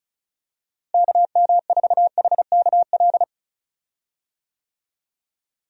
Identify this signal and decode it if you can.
Morse code, heard as a keyed tone, about 36 wpm, 700 Hz. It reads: KM4HKL
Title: USA Call Signs - 6 Characters - Repeated 2x - 35wpm